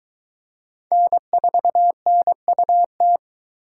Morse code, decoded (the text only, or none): N4NUT